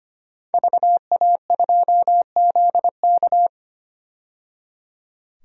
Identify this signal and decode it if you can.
Morse code, heard as a keyed tone, about 25 wpm, 700 Hz. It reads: VA2ZK
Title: DXCC-100 Call Signs - 5 Characters - 25wpm